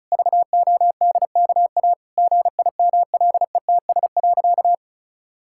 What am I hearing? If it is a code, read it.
Morse code, heard as a keyed tone, about 36 wpm, 700 Hz. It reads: VODKA GIMLETS.
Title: Pangram Challenge v2 - 35wpm